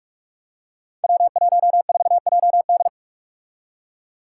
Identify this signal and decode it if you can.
Morse code, heard as a keyed tone, about 46 wpm, 700 Hz. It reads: W14JD
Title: Five Challenging Letter-Number Combinations 45wpm